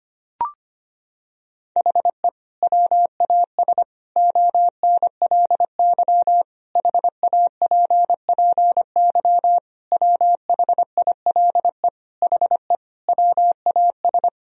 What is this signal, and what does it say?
Morse code, heard as a keyed tone, about 25 wpm, 700 Hz. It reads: HE WAS ONLY HAPPY WHILE HE WAS
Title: Sentences from Top 1000 Words - Review Part 4 - 25wpm